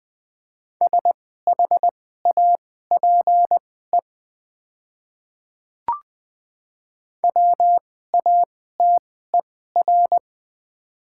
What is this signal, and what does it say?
Morse code, heard as a keyed tone, about 20 wpm, spaced as for 15 wpm, 700 Hz. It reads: SHAPE WATER
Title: 2x Character Spacing - Top 500 Words - 20wpm